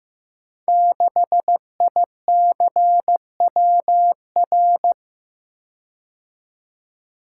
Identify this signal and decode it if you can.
Morse code, heard as a keyed tone, about 15 wpm, 700 Hz. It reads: 6ICWR
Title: Code Groups - 5 Letter:Numbers - 15wpm